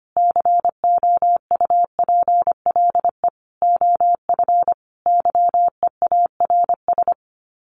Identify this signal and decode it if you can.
Morse code, heard as a keyed tone, about 25 wpm, 700 Hz. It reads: COUPLE OF YEARS
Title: Sets of 5 Words 25wpm